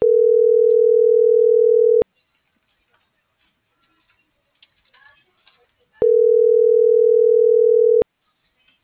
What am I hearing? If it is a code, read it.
Ringback tone (North America), 440 + 480 Hz.